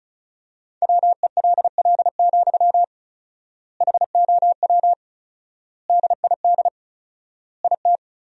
Morse code, heard as a keyed tone, about 36 wpm, 700 Hz. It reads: WELL, HOW DID IT
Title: 4x Word-Spacing - Sentences from Top 200 Words - 35wpm